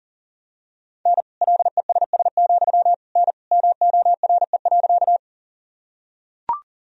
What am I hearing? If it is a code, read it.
Morse code, heard as a keyed tone, about 40 wpm, 700 Hz. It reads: N LESS, N MORE.
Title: Sentences from Top 400 Words and Abbreviations 40wpm